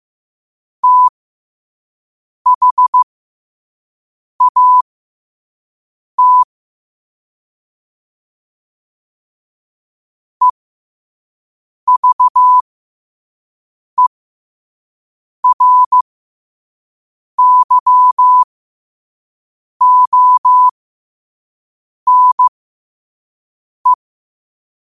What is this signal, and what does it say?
Morse code, heard as a keyed tone, about 15 wpm, spaced as for 5 wpm, 1000 Hz. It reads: THAT EVERYONE
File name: qotdfiles / 5wpm